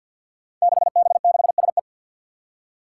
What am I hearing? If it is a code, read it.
Morse code, heard as a keyed tone, about 50 wpm, 700 Hz. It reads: 6B6SE
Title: Five Challenging Letter-Number Combinations 50wpm